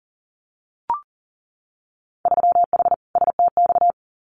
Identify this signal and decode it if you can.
Morse code, heard as a keyed tone, about 40 wpm, 700 Hz. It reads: 3H STX
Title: Winter Field Day Exchanges Encore 40wpm